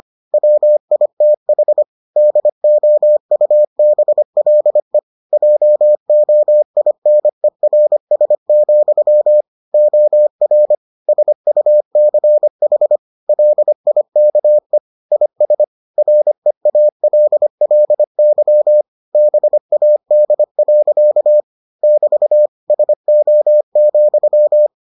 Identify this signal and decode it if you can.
Morse code, heard as a keyed tone, about 25 wpm, 600 Hz. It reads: WITH DOUBLE JOINERS, OR SUCH LIKE IS REALLY BAD. <BT> SO,